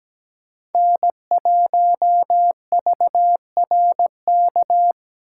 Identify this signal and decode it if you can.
Morse code, heard as a keyed tone, about 17 wpm, 700 Hz. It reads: N1VRK
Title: USA Call Signs - 5 Characters - 8x Character Spacing - 17wpm